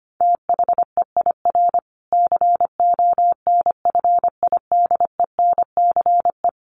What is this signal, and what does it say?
Morse code, heard as a keyed tone, about 25 wpm, 700 Hz. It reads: THEIR CONFIDENCE
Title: Sentences from Top 1500 Words 25wpm